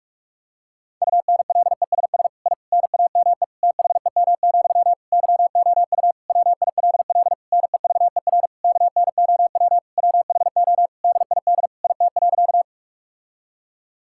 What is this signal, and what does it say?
Morse code, heard as a keyed tone, about 41 wpm, 700 Hz. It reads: UNLESS I NAME THEM, YOU WILL NEVER KNOW WHO DID IT.
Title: Sentences from Top 1000 Words - Review Part 6 - 45wpm